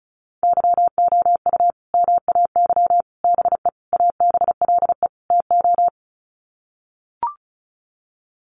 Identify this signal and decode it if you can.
Morse code, heard as a keyed tone, about 34 wpm, 700 Hz. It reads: YOU MAY BE ABLE TO
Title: Sets of 5 Words Encore 35wpm